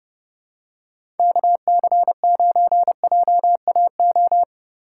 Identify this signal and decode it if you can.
Morse code, heard as a keyed tone, about 31 wpm, 700 Hz. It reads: KC9JAO
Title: USA Call Signs - 6 Characters - 2x Character Spacing - 30wpm